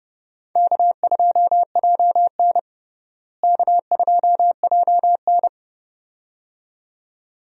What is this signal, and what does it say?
Morse code, heard as a keyed tone, about 31 wpm, 700 Hz. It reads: K2JN K2JN
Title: USA Call Signs - 4 Characters - Repeated 2x - 30wpm